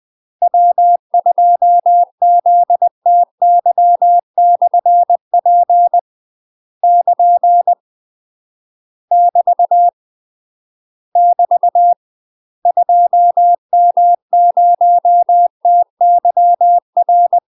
Morse code, heard as a keyed tone, about 20 wpm, 700 Hz. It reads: W2ZTY/P <KN> <BT> <BT> 2M0TYR
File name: UK MASTER QSOs TRAINING / 20WPM / UK MASTER QSOs 0064